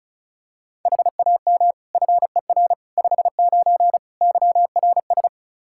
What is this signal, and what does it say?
Morse code, heard as a keyed tone, about 36 wpm, 700 Hz. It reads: HAM FER 59 YRS
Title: Practice QSO - Exchange 3 - More Info - 35wpm